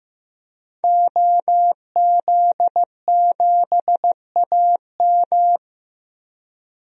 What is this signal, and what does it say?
Morse code, heard as a keyed tone, about 15 wpm, 700 Hz. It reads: OZ7AM
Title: DXCC-100 Call Signs - 5 Characters - 15wpm